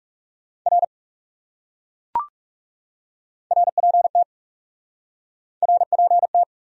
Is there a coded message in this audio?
Yes: Morse code from 0.7 to 6.4 s, heard as a keyed tone, about 40 wpm, 700 Hz. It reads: R RPT RPT